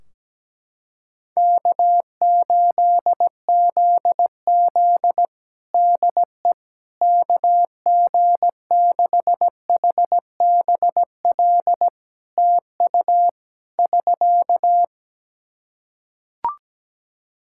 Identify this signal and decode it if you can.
Morse code, heard as a keyed tone, about 17 wpm, 700 Hz. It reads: K8ZZ DE KG6HBL TU <SK>
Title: Practice QSO - Exchange 4 - Ending - 17wpm